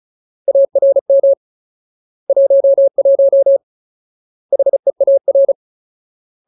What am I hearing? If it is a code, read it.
Morse code, heard as a keyed tone, about 34 wpm, 550 Hz. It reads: ARM 11 HEAR
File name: Ultimate Word List / 35WPM Ultimate